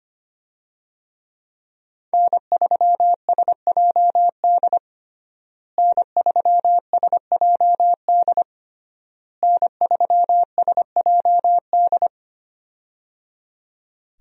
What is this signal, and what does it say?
Morse code, heard as a keyed tone, about 25 wpm, 700 Hz. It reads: N3SJD N3SJD N3SJD
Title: USA Call Signs - 5 Characters - Repeated 3x - 25wpm